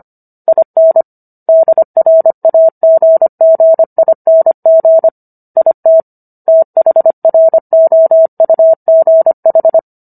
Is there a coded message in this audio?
Yes: Morse code from 0.5 to 9.8 s, heard as a keyed tone, about 25 wpm, 650 Hz. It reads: IN DRAGGING IT THROUGH